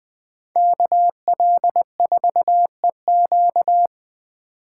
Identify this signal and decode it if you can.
Morse code, heard as a keyed tone, about 19 wpm, 700 Hz. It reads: KL4EQ
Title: Speed Racing - USA 2x2 Call Signs 17wpm